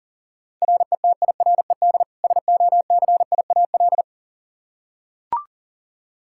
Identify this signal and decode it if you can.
Morse code, heard as a keyed tone, about 40 wpm, 700 Hz. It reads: RETIRED SOCIAL
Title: Practice QSO - Exchange 3 - More Info - 40wpm